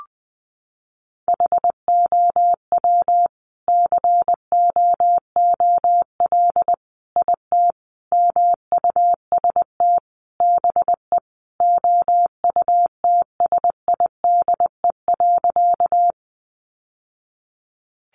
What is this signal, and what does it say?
Morse code, heard as a keyed tone, about 19 wpm, 700 Hz. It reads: HOW COOL IT MUST BE OUTSIDE.
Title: Sentences from Top 1200 Words 20wpm